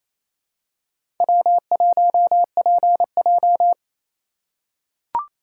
Morse code, heard as a keyed tone, about 28 wpm, 700 Hz. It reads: W1PJ
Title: USA Call Signs - 4 Characters - 2x Character Spacing - 28wpm